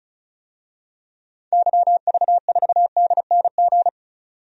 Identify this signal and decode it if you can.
Morse code, heard as a keyed tone, about 36 wpm, 700 Hz. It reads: YV4DNG
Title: DXCC-100 Call Signs - 6 Characters - 35wpm